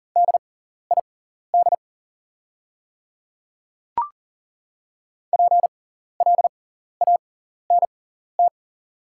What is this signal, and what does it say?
Morse code, heard as a keyed tone, about 40 wpm, spaced as for 14 wpm, 700 Hz. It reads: DID PLANT
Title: 6x Character Spacing - Top 300 Words - 40wpm